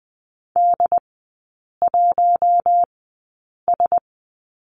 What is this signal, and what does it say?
Morse code, heard as a keyed tone, about 20 wpm, spaced as for 8 wpm, 700 Hz. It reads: D1S
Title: 100 Lesson 17e - Letter u - Three Characters 20wpm